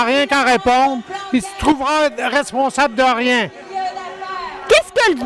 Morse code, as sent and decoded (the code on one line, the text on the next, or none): none